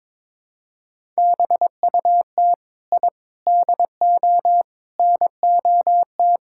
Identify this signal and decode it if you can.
Morse code, heard as a keyed tone, about 22 wpm, 700 Hz. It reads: BUT I DO NOT
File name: practice / 22wpm / Build A Phrase - 5-Word Phrases - 22wpm